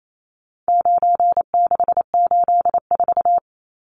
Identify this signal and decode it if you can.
Morse code, heard as a keyed tone, about 28 wpm, 700 Hz. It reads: 9684
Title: Numbers - 4-Digit 28wpm